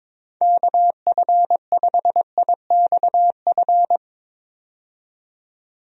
Morse code, heard as a keyed tone, about 21 wpm, 700 Hz. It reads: KF5IXF
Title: USA Call Signs - 6 Characters - 22wpm